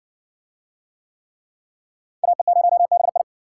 Morse code, heard as a keyed tone, about 60 wpm, 700 Hz. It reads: AE0BI